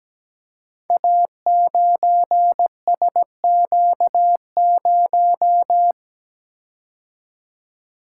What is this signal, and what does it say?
Morse code, heard as a keyed tone, about 17 wpm, 700 Hz. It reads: A9SQ0
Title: Code Groups - 5 Letter:Numbers - 17wpm